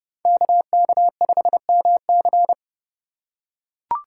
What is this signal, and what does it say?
Morse code, heard as a keyed tone, about 29 wpm, 700 Hz. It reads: KK5MC